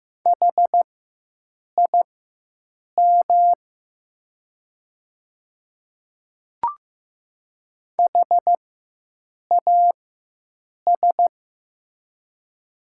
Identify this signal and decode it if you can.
Morse code, heard as a keyed tone, about 15 wpm, spaced as for 7 wpm, 700 Hz. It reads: HIM HAS
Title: 4x Character Spacing - Top 300 Words - 15wpm